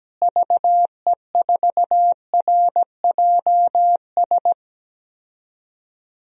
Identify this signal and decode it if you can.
Morse code, heard as a keyed tone, about 17 wpm, 700 Hz. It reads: VE4RJS
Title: DXCC-100 Call Signs - 6 Characters - Repeated 3x - 17wpm